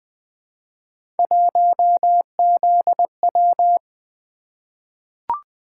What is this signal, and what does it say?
Morse code, heard as a keyed tone, about 20 wpm, 700 Hz. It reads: 1ZW
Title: Three Letter-Number Combinations - 20wpm